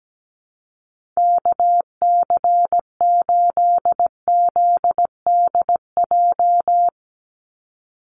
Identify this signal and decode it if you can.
Morse code, heard as a keyed tone, about 17 wpm, 700 Hz. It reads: KC8ZDJ
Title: USA Call Signs 17wpm